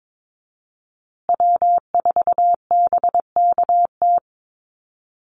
Morse code, heard as a keyed tone, about 22 wpm, 700 Hz. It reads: W4BKT
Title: USA Call Signs Repeated 2x 22wpm